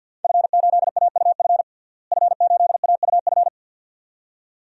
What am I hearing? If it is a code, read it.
Morse code, heard as a keyed tone, about 47 wpm, 700 Hz. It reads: F8AUF F8AUF
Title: DXCC-100 Call Signs - 5 Characters - Repeated 2x - 50wpm